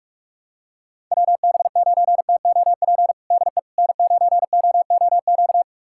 Morse code, heard as a keyed tone, about 41 wpm, 700 Hz. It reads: WB9TOP DE N9OOQ
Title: USA Callsign DE Callsign 45wpm